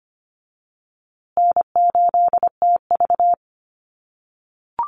Morse code, heard as a keyed tone, about 25 wpm, 700 Hz. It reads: N8TV
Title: USA Call Signs Repeated 3x 25wpm